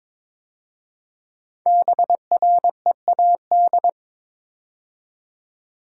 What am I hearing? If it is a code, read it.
Morse code, heard as a keyed tone, about 22 wpm, 690 Hz. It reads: BREAD
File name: 5 Letter Words - First 3 Letters Given - 22wpm